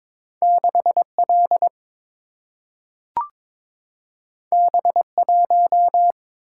Morse code, heard as a keyed tone, about 22 wpm, 700 Hz. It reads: 6L B1
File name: practice / 22wpm / Two Letter-Number Combinations - 22wpm